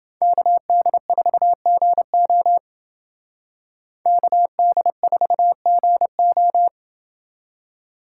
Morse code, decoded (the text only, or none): KD4GO KD4GO